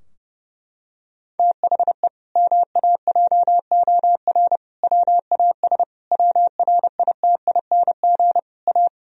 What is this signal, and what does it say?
Morse code, heard as a keyed tone, about 32 wpm, 700 Hz. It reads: THE MAJOR WAS WRITING A
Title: Sentences from Top 1000 Words - Review Part 4 - 30wpm